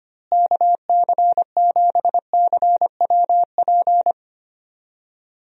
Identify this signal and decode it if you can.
Morse code, heard as a keyed tone, about 25 wpm, 700 Hz. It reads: KC7CWP
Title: USA Call Signs - 6 Characters - Repeated 3x - 25wpm